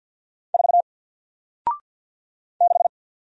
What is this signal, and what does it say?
Morse code, heard as a keyed tone, about 48 wpm, 700 Hz. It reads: 4 6
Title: Single Character (Letter:Number:Punctuation:Prosign) Alt 50wpm